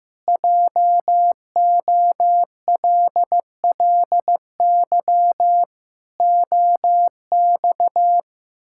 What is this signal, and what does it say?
Morse code, heard as a keyed tone, about 15 wpm, 700 Hz. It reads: JOLLY OX
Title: Pangram Challenge v2 - 15wpm